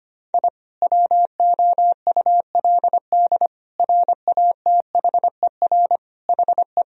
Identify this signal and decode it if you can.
Morse code, heard as a keyed tone, about 25 wpm, 690 Hz. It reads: I WOULD RATHER HE
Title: Sentences from Top 1000 Words - Review Part 4 - 25wpm